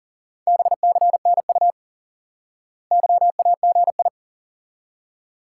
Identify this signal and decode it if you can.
Morse code, heard as a keyed tone, about 40 wpm, 700 Hz. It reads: BCNU YAGI
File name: IQE Course 3 - Lesson 89 - YAGI - Two Elements - 40wpm